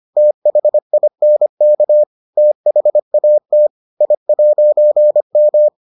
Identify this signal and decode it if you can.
Morse code, heard as a keyed tone, about 25 wpm, 600 Hz. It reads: THINK THAT I'M